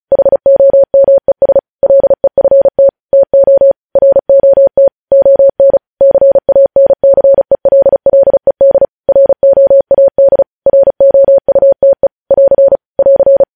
Morse code, heard as a keyed tone, about 36 wpm, 550 Hz. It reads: HOMES LEFT TO ROT ON CANCELLED ROAD ROUTE <AR> <AR>
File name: podcast / News headlines-1x-35x35